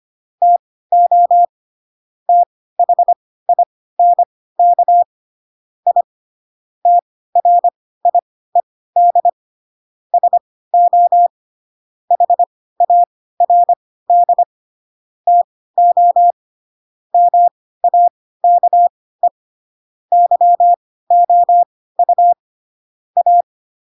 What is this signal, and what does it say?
Morse code, heard as a keyed tone, about 25 wpm, spaced as for 16 wpm, 700 Hz. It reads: TO THINK I TRIED SO HARD TO MAKE YOU A
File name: Pinocchio / Pinocchio ch3 10 25 ik1hge